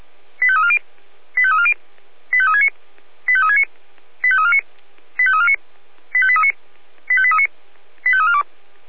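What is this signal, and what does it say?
Five-tone selective call (ZVEI), heard as quick runs of five tones, tones of about 70 ms. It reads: 85340 85340 85369 85369 85349 85349 85839 85839 85342
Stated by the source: ZVEI five-tone selective call received on an SDR.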